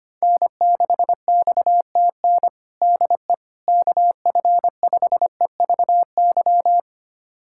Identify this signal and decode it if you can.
Morse code, heard as a keyed tone, about 25 wpm, 700 Hz. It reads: N6XTN DE KF5EVY
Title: USA Callsign DE Callsign 25wpm